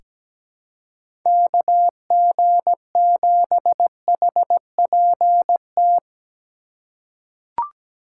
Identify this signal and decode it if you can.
Morse code, heard as a keyed tone, about 17 wpm, 700 Hz. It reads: KG7HPT
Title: USA Call Signs - 6 Characters - 4x Character Spacing - 17wpm